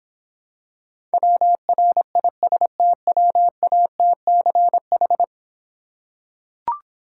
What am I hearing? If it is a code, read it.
Morse code, heard as a keyed tone, about 25 wpm, 700 Hz. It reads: WRISTWATCH
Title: Difficult Words - Speed-Racing with Hints - 20wpm